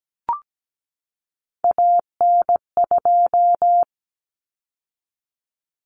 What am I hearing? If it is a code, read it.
Morse code, heard as a keyed tone, about 17 wpm, 700 Hz. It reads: AN2
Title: Common Cut Numbers - 3-Digit 17wpm